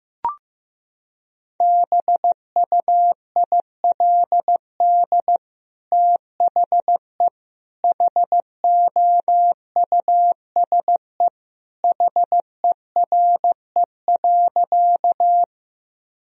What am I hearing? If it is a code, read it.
Morse code, heard as a keyed tone, about 15 wpm, 700 Hz. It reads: BUILD THE HOUSE HERE.
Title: Sentences from Top 1000 Words - Review Part 1 - 15wpm